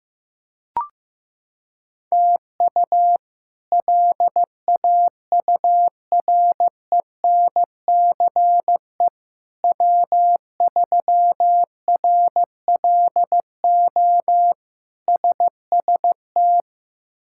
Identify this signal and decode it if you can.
Morse code, heard as a keyed tone, about 15 wpm, 700 Hz. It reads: TU LAURENCE W3RLO SST
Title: K1USN SST QSO Practice - 15wpm